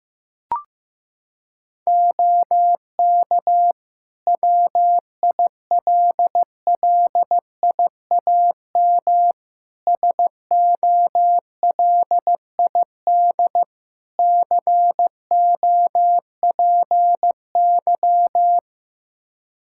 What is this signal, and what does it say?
Morse code, heard as a keyed tone, about 16 wpm, 700 Hz. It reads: OK WILLIAM SOLID COPY
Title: Practice QSO - Exchange 3 - More Info - 15wpm